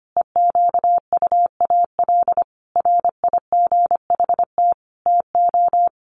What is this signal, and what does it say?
Morse code, heard as a keyed tone, about 25 wpm, 700 Hz. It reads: EQUAL RIGHT TO
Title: Sentences from Top 1400 Words 25wpm